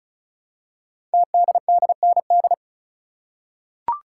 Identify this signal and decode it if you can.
Morse code, heard as a keyed tone, about 36 wpm, 700 Hz. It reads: TDDND
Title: Code Groups - 5 Letters - 35wpm